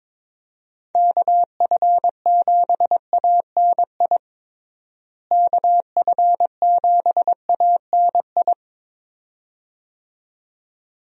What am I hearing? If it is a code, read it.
Morse code, heard as a keyed tone, about 22 wpm, 700 Hz. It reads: KF7ANI KF7ANI
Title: USA Call Signs - 6 Characters - Repeated 2x - 22wpm